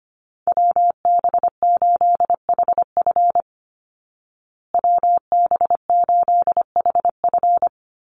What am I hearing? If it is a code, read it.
Morse code, heard as a keyed tone, about 25 wpm, 700 Hz. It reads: WB8HF WB8HF
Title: Call Signs Repeated 3x - 25wpm